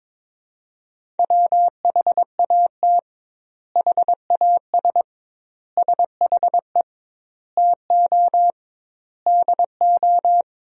Morse code, heard as a keyed tone, about 22 wpm, 700 Hz. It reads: WHAT HAS SHE TO DO
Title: 2x Word-Spacing - Sentences from Top 100 Words - 22wpm